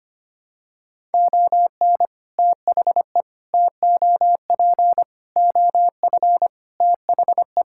Morse code, heard as a keyed tone, about 25 wpm, 700 Hz. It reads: ON THE TOP OF THE
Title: Sentences from Top 1000 Words - Review Part 6 - 25wpm